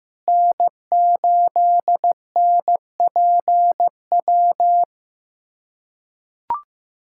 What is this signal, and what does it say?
Morse code, heard as a keyed tone, about 15 wpm, 700 Hz. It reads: N8NPW
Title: USA Callsign DE Callsign - 5 Characters - 15wpm